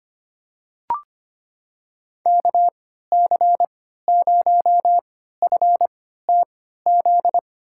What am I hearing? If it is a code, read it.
Morse code, heard as a keyed tone, about 25 wpm, spaced as for 14 wpm, 700 Hz. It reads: KC0FTZ
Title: USA Call Signs - 6 Characters - 3x Character Spacing - 25wpm